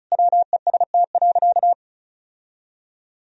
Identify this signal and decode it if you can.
Morse code, heard as a keyed tone, about 36 wpm, 700 Hz. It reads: WEST.
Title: Sentences from Top 1000 Words - Review Part 2 - 35wpm